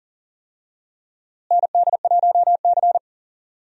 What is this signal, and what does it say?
Morse code, heard as a keyed tone, about 40 wpm, 700 Hz. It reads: ND1C
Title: USA Call Signs - 4 Characters - Repeated 3x - 40wpm